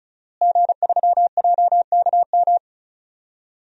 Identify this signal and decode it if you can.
Morse code, heard as a keyed tone, about 36 wpm, 700 Hz. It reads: G3JKM
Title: DXCC-100 Call Signs - 5 Characters - Repeated 2x - 35wpm